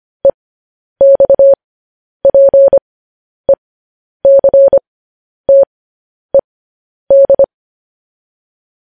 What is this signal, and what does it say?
Morse code, heard as a keyed tone, about 25 wpm, spaced as for 10 wpm, 550 Hz. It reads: EXPECTED
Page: news headlines in morse code at 10 wpm